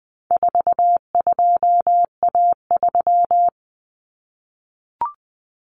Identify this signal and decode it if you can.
Morse code, heard as a keyed tone, about 20 wpm, 700 Hz. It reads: 42A3
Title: Common Cut Numbers - 4-Digit 20wpm